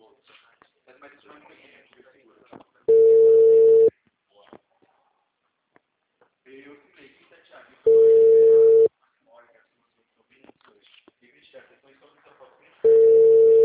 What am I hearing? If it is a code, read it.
Ringback tone (China, 450 Hz), with speech.